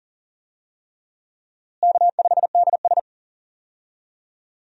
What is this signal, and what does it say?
Morse code, heard as a keyed tone, about 40 wpm, 700 Hz. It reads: K5DS
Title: USA Call Signs - 4 Characters - Repeated 3x - 40wpm